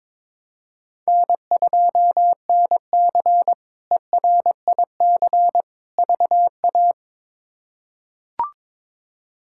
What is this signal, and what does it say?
Morse code, heard as a keyed tone, about 21 wpm, 700 Hz. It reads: N2NC ERIC VA
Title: K1USN SST QSO Practice - 22wpm